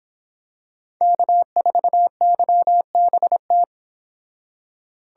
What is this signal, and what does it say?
Morse code, heard as a keyed tone, about 26 wpm, 700 Hz. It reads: K4YBT